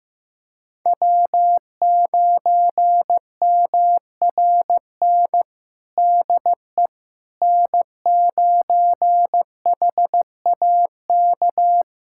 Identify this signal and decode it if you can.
Morse code, heard as a keyed tone, about 15 wpm, 700 Hz. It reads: W9MRN DE N9HAK
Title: USA Callsign DE Callsign - 5 Characters - 15wpm